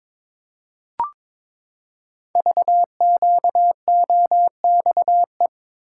Morse code, heard as a keyed tone, about 22 wpm, 700 Hz. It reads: VQOXE
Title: Code Groups - 5 Letters - 22wpm